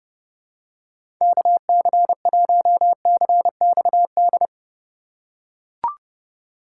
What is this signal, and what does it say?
Morse code, heard as a keyed tone, about 30 wpm, 700 Hz. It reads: KC1CXD
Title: USA Call Signs - 6 Characters - 3x Character Spacing - 30wpm